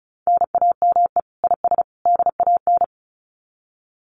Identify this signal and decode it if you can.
Morse code, heard as a keyed tone, about 34 wpm, 700 Hz. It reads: NAME IS DAN
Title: Common QSO Phrases 35wpm